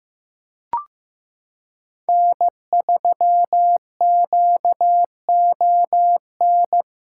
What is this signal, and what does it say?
Morse code, heard as a keyed tone, about 15 wpm, 700 Hz. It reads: N3QON
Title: USA Callsign DE Callsign - 5 Characters - 15wpm